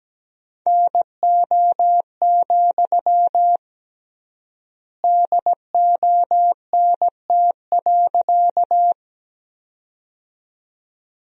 Morse code, heard as a keyed tone, about 17 wpm, 700 Hz. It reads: NO, DONT.
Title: 3x Word-Spacing - Sentences from Top 300 Words - 17wpm